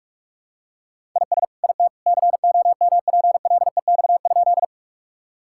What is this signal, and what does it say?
Morse code, heard as a keyed tone, about 46 wpm, 700 Hz. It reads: IS IT COMPLEX?
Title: Sentences from Top 1000 Words - Review Part 6 - 45wpm